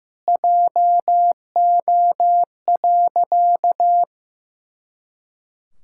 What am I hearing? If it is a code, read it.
Morse code, heard as a keyed tone, about 15 wpm, 700 Hz. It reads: JO.
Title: Pangram Challenge v2 - 15wpm